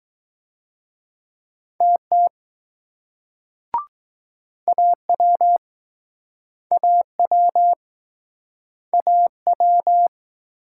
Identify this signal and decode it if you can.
Morse code, heard as a keyed tone, about 21 wpm, 700 Hz. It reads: TT AW AW AW